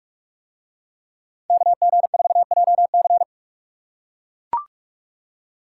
Morse code, heard as a keyed tone, about 41 wpm, 700 Hz. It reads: KG4JC